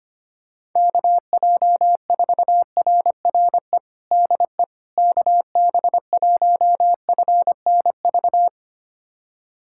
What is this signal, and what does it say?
Morse code, heard as a keyed tone, about 25 wpm, 700 Hz. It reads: KJ4RRE DE KB1FNV
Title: USA Callsign DE Callsign - 6 Characters - 25wpm